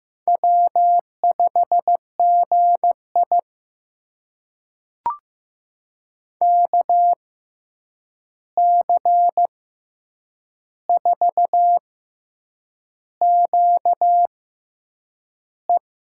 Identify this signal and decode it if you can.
Morse code, heard as a keyed tone, about 15 wpm, spaced as for 7 wpm, 700 Hz. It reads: W5GI KC4QE
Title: USA Call Signs - 6x Character Spacing - 15wpm